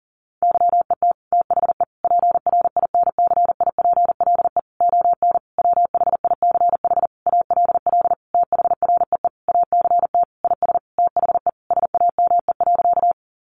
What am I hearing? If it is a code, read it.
Morse code, heard as a keyed tone, about 37 wpm, 700 Hz. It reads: YET THE PRINCIPLE ON WHICH ALL THREE ACT IS THE SAME.
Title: Sentences from Top 1400 Words 40wpm